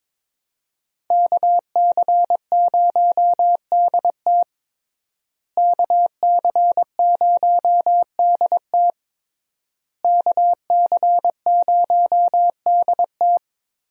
Morse code, heard as a keyed tone, about 21 wpm, 700 Hz. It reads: KC0DT KC0DT KC0DT
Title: USA Call Signs - 5 Characters - Repeated 3x - 22wpm